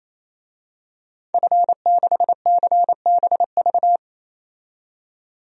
Morse code, heard as a keyed tone, about 28 wpm, 700 Hz. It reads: F6CBV
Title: DXCC-100 Call Signs - 5 Characters - 28wpm